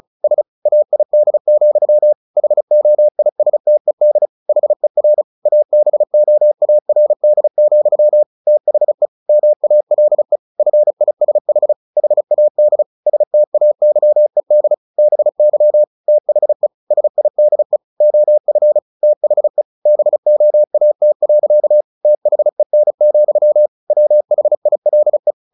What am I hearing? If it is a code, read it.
Morse code, heard as a keyed tone, about 34 wpm, 600 Hz. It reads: SAID, HOISTED HER ABOARD, THE MALE FISH HAD STAYED BY THE SIDE OF THE BOAT. THEN, WHILE
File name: the old man and the sea / the old man and the sea 1 35wpm